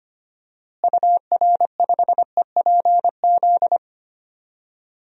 Morse code, heard as a keyed tone, about 25 wpm, 700 Hz. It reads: UR5EPZ
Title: DXCC-100 Call Signs - 6 Characters - Repeated 2x - 25wpm